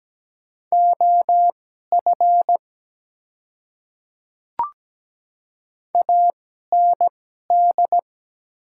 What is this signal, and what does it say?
Morse code, heard as a keyed tone, about 17 wpm, spaced as for 12 wpm, 690 Hz. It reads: OF AND
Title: 2x Character Spacing - Top 100 Words - 17wpm